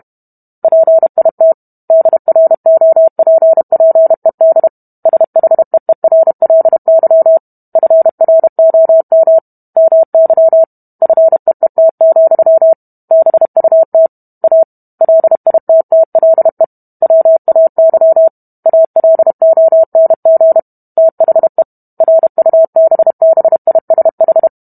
Morse code, heard as a keyed tone, about 32 wpm, 650 Hz. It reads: PIT DROPPED SHEERLY FROM MY FEET, BUT A LITTLE WAY ALONG THE RUBBISH